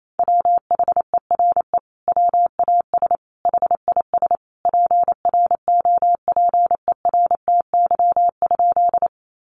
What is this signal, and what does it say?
Morse code, heard as a keyed tone, about 28 wpm, 700 Hz. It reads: WHERE WAS HIS PROPERTY?
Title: Sentences from Top 1000 Words 28wpm